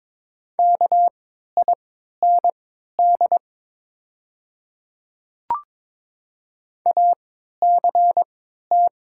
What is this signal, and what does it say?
Morse code, heard as a keyed tone, about 22 wpm, spaced as for 12 wpm, 700 Hz. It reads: KIND ACT
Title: 3x Character Spacing - Top 300 Words - 22wpm